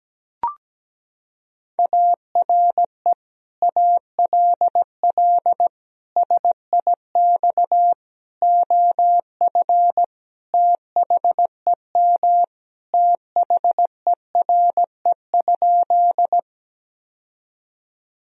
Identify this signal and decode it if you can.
Morse code, heard as a keyed tone, about 17 wpm, 700 Hz. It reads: ARE ALL SIX OF THEM THERE?
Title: Sentences from Top 1000 Words - Review Part 2 - 17wpm